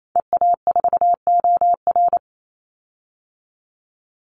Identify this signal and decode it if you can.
Morse code, heard as a keyed tone, about 28 wpm, 700 Hz. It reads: EA4OR
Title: Call Signs Repeated 2x - 28wpm